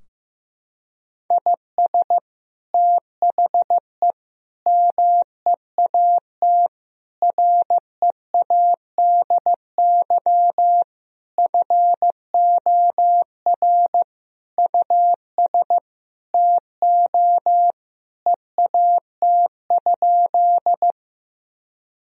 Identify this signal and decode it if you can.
Morse code, heard as a keyed tone, about 15 wpm, 700 Hz. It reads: IS THE MEAT READY FOR US TO EAT?
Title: Sentences from Top 1000 Words - Review Part 3 - 15wpm